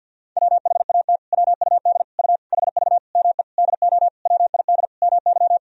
Morse code, heard as a keyed tone, about 52 wpm, 700 Hz. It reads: WHAT WUD U HV ME DO WID MY
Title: Sentences from Top 300 Words and Abbreviations 50wpm